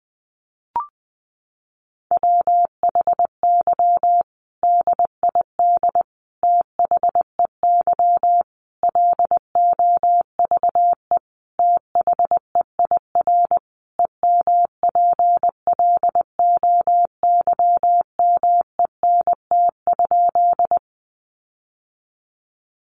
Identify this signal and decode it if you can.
Morse code, heard as a keyed tone, about 20 wpm, 700 Hz. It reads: WHY DID THEY LOVE THEIR EMPLOYMENT?
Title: Sentences from Top 1500 Words 20wpm